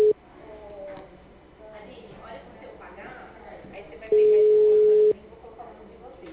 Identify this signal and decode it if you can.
Ringback tone (Europe, 425 Hz), with speech.
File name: out- -2019- - - .1035